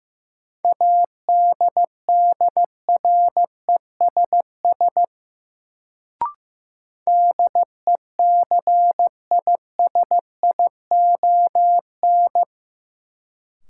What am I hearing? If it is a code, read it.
Morse code, heard as a keyed tone, about 15 wpm, 700 Hz. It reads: ADDRESS DECISION
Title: Sentences from Top 1000 Words - New Word Drill - 15wpm